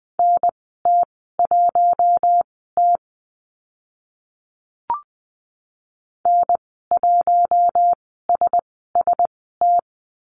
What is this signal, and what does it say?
Morse code, heard as a keyed tone, about 20 wpm, spaced as for 14 wpm, 700 Hz. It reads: NT1T N1SST
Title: 048 Lesson 9g - Number 4 - Call Signs 20wpm